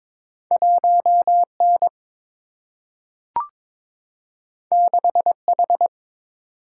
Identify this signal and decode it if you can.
Morse code, heard as a keyed tone, about 22 wpm, 700 Hz. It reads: 1 N 6 H
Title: Single-Letter Single-Number 22wpm